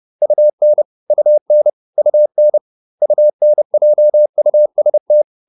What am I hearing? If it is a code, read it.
Morse code, heard as a keyed tone, about 30 wpm, 600 Hz. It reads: UN UN UN UNJUST
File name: UN6 30